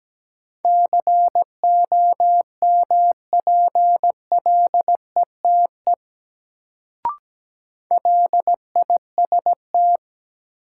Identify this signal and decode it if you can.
Morse code, heard as a keyed tone, about 17 wpm, 700 Hz. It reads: COMPLETE LIST
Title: Sentences from Top 400 Words - New Word Drill - 17wpm